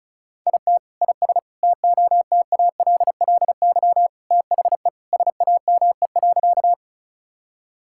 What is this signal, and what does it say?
Morse code, heard as a keyed tone, about 36 wpm, 700 Hz. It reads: IT IS TOTALLY THE SAME.
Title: Sentences from Top 1600 Words 35wpm